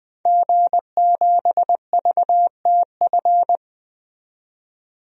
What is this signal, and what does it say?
Morse code, heard as a keyed tone, about 20 wpm, 700 Hz. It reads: G7VTF
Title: DXCC-100 Call Signs - 5 Characters - 20wpm